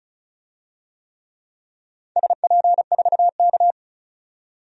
Morse code, heard as a keyed tone, about 36 wpm, 690 Hz. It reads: SP4K
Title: DXCC-100 Call Signs - 4 Characters - 35wpm